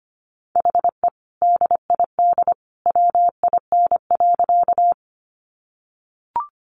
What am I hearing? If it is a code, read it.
Morse code, heard as a keyed tone, about 25 wpm, 700 Hz. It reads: HE DID WIN.
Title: Sentences from Top 1500 Words 25wpm